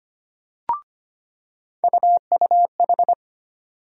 Morse code, heard as a keyed tone, about 25 wpm, 700 Hz. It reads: UUH
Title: USA Call Sign Postfixes - 3 Letters 25wpm